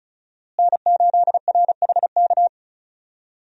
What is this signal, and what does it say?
Morse code, heard as a keyed tone, about 34 wpm, 700 Hz. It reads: N8RHK
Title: USA Call Signs - 5 Characters - Repeated 3x - 35wpm